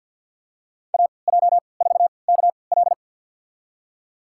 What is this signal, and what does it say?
Morse code, heard as a keyed tone, about 53 wpm, spaced as for 29 wpm, 700 Hz. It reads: AJ4KL
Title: USA Call Signs - 5 Characters - 3x Character Spacing - 50wpm